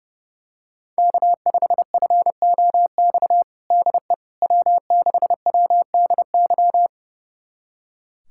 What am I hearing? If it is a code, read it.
Morse code, heard as a keyed tone, about 32 wpm, 700 Hz. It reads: K5FOX DE W6WDY
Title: USA Callsign DE Callsign - 5 Characters - 30wpm